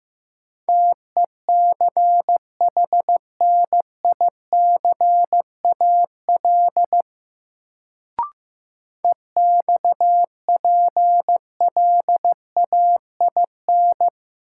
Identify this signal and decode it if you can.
Morse code, heard as a keyed tone, about 15 wpm, 700 Hz. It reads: TECHNICAL EXPLAIN
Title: Sentences from Top 1100 Words - New Word Drill - 15wpm